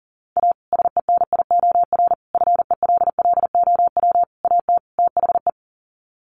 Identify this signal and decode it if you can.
Morse code, heard as a keyed tone, about 40 wpm, 700 Hz. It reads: A SENIOR FELLOW AT THE
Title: Sets of 5 Words Encore 40wpm